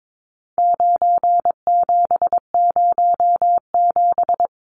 Morse code, heard as a keyed tone, about 22 wpm, 700 Hz. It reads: 9707
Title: Numbers - 4-Digit 22wpm